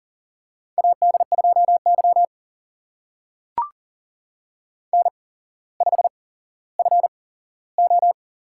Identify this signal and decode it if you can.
Morse code, heard as a keyed tone, about 40 wpm, spaced as for 17 wpm, 700 Hz. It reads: AD2Y N5FO
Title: USA Call Signs - 4 Characters - 8x Character Spacing - 40wpm